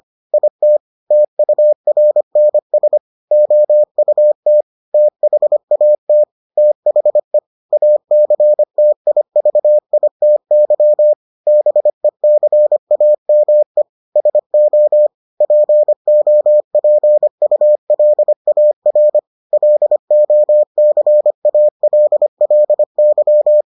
Morse code, heard as a keyed tone, about 25 wpm, 600 Hz. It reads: IT TURNS OUT THAT THE ACTIVITY BECAME SO POPULAR LOCALLY